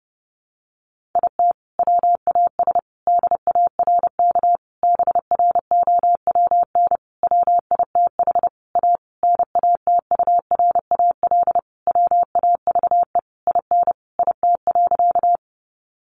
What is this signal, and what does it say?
Morse code, heard as a keyed tone, about 28 wpm, 700 Hz. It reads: IT WAS DARK BROWN WITH A NATURAL WAVE IN IT.
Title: Sentences from Top 1400 Words 30wpm